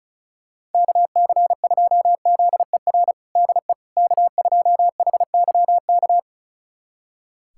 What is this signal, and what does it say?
Morse code, heard as a keyed tone, about 36 wpm, 700 Hz. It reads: KC2ZER DE K2HYK
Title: USA Callsign DE Callsign 35wpm